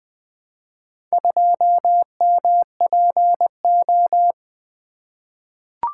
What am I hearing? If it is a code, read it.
Morse code, heard as a keyed tone, about 20 wpm, 700 Hz. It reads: 2MPO
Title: Four Letter-Number Combinations - 20wpm